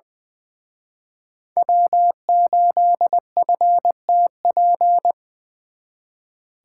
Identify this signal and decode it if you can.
Morse code, heard as a keyed tone, about 23 wpm, 700 Hz. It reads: W8FTP
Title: Speed Racing - USA 1x3 Call Signs 20wpm